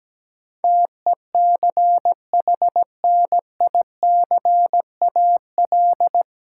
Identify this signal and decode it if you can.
Morse code, heard as a keyed tone, about 17 wpm, 700 Hz. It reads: TECHNICAL
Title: Sentences from Top 1100 Words - New Word Drill - 17wpm